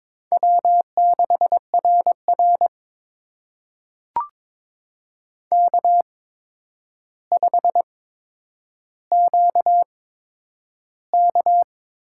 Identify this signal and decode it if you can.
Morse code, heard as a keyed tone, about 23 wpm, spaced as for 10 wpm, 700 Hz. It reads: W6RR K5QK
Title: USA Call Signs - 4 Characters - 8x Character Spacing - 22wpm